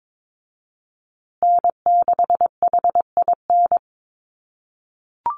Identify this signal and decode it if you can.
Morse code, heard as a keyed tone, about 22 wpm, 700 Hz. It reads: N6HIN
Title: USA Call Signs Repeated 2x 22wpm